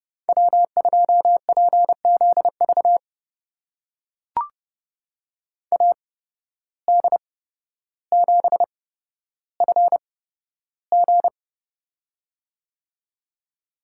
Morse code, heard as a keyed tone, about 29 wpm, spaced as for 13 wpm, 700 Hz. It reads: W2PZV AD7FG
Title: USA Call Signs - 5 Characters - 8x Character Spacing - 30wpm